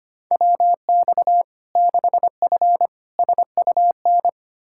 Morse code, heard as a keyed tone, about 25 wpm, 700 Hz. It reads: WX 6F SUN
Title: Practice QSO - Exchange 2 - Info - 25wpm